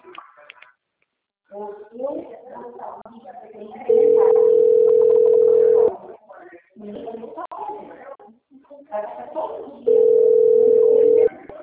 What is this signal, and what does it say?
Ringback tone (North America), 440 + 480 Hz, with speech.
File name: out- - - - - .51141